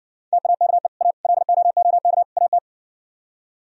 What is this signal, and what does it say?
Morse code, heard as a keyed tone, about 62 wpm, 700 Hz. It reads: TAKE A LOOK AT